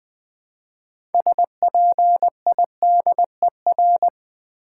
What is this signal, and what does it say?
Morse code, heard as a keyed tone, about 20 wpm, 700 Hz. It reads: SPIDER
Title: 6 Letter Words - First 3 Letters Given - 20wpm